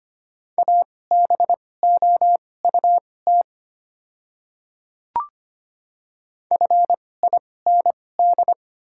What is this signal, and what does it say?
Morse code, heard as a keyed tone, about 25 wpm, spaced as for 18 wpm, 690 Hz. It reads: ABOUT FIND
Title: 2x Character Spacing - Top 100 Words - 25wpm